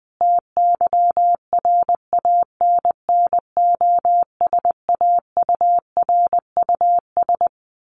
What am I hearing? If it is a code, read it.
Morse code, heard as a keyed tone, about 20 wpm, 700 Hz. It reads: TYRANNOSAURUS
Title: A Sound Of Thunder 20wpm